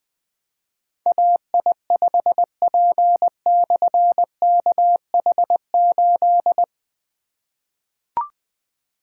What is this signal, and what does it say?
Morse code, heard as a keyed tone, about 20 wpm, 700 Hz. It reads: AI5P/KH8
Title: Difficult Call Signs 20wpm